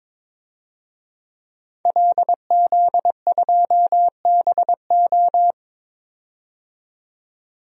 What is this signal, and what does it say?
Morse code, heard as a keyed tone, about 22 wpm, 700 Hz. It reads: LZ2BO
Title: DXCC-100 Call Signs - 5 Characters - 22wpm